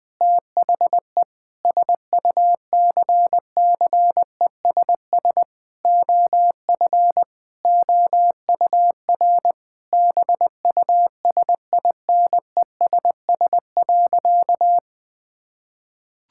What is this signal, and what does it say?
Morse code, heard as a keyed tone, about 21 wpm, 700 Hz. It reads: THE SUCCESS OF OUR BUSINESS.
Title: Sentences from Top 1000 Words - Review Part 6 - 20wpm